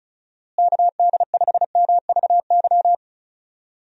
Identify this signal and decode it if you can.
Morse code, heard as a keyed tone, about 36 wpm, 700 Hz. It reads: KD5MVY
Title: USA Call Signs - 6 Characters - Repeated 2x - 35wpm